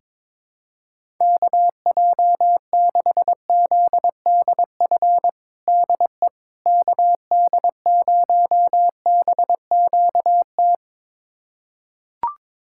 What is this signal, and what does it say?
Morse code, heard as a keyed tone, about 22 wpm, 700 Hz. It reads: KJ6ZDF DE KD0BQT
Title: USA Callsign DE Callsign - 6 Characters - 22wpm